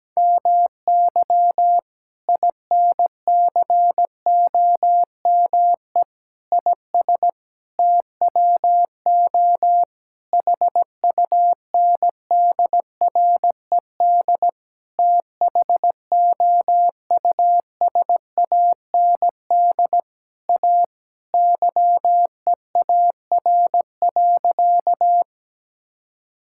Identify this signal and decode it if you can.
Morse code, heard as a keyed tone, about 17 wpm, 700 Hz. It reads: MY INCOME IS TWO HUNDRED THOUSAND A YEAR.
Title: Sentences from Top 1000 Words - Review Part 5 - 17wpm